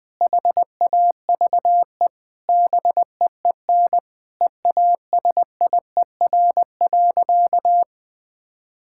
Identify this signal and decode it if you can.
Morse code, heard as a keyed tone, about 20 wpm, 700 Hz. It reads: HAVE BEEN EASIER.
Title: Sentences with 4-Word Phrases - 20wpm